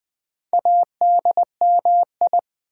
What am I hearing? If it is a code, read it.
Morse code, heard as a keyed tone, about 20 wpm, 700 Hz. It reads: ADMI
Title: Build A Word - 4 Letters At A Time - 10-Letter Words - 20wpm